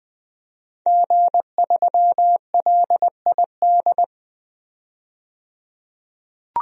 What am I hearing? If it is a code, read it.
Morse code, heard as a keyed tone, about 20 wpm, 700 Hz. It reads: G3LID
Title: DXCC-100 Call Signs - 5 Characters - 20wpm